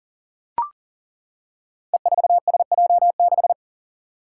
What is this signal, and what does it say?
Morse code, heard as a keyed tone, about 38 wpm, 700 Hz. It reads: E4SJ6
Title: Five Challenging Letter-Number Combinations 40wpm